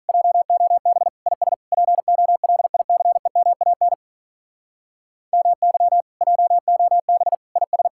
Morse code, heard as a keyed tone, about 44 wpm, 700 Hz. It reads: JOB IS POLICEMAN MY JOB IS